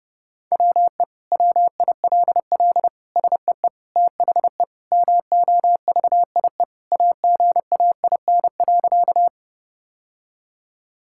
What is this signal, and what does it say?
Morse code, heard as a keyed tone, about 30 wpm, 700 Hz. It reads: WE WILL SEE THE MOVIE AGAIN.
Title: Sentences from Top 1000 Words - Review Part 5 - 30wpm